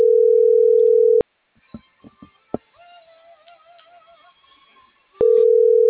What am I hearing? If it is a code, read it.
Ringback tone (North America), 440 + 480 Hz.